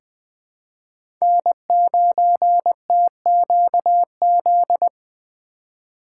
Morse code, heard as a keyed tone, about 20 wpm, 700 Hz. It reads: N9TQZ
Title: USA Call Signs - 5 Characters - 2x Character Spacing - 20wpm